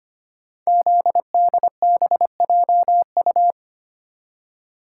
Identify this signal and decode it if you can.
Morse code, heard as a keyed tone, about 25 wpm, 700 Hz. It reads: ZDBJU
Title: Code Groups - 5 Letters - 25wpm